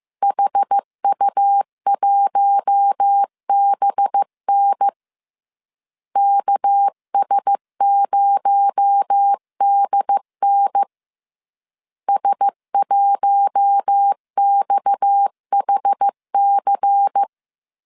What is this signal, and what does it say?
Morse code, heard as a keyed tone, about 15 wpm, 800 Hz. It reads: HU1BN KS0DN S1XHC